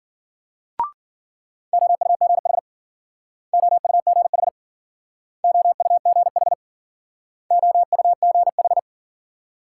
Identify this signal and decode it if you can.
Morse code, heard as a keyed tone, about 51 wpm, 700 Hz. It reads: OUGH OUGH OUGH OUGH